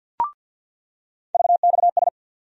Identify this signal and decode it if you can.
Morse code, heard as a keyed tone, about 46 wpm, 700 Hz. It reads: VXS
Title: USA Call Sign Postfixes - 3 Letters 50wpm